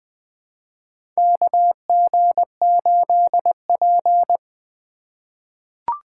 Morse code, heard as a keyed tone, about 21 wpm, 700 Hz. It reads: KG8P
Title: USA Call Signs - 3x Character Spacing - 20wpm